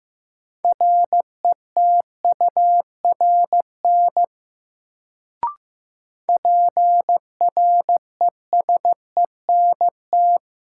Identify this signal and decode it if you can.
Morse code, heard as a keyed tone, about 15 wpm, 700 Hz. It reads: RETURN PRESENT
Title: Sentences from Top 800 Words - New Word Drill - 15wpm